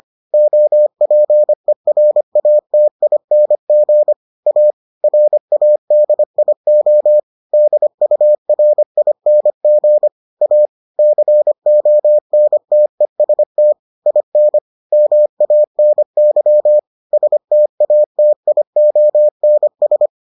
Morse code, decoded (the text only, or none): OPERATING A RADIO DURING A CONTEST IN MANY STATIONS